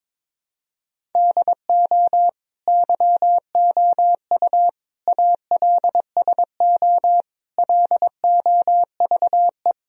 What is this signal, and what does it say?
Morse code, heard as a keyed tone, about 22 wpm, 700 Hz. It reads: DO YOU ALSO LOVE
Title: Sentences from Top 1000 Words - Review Part 1 - 22wpm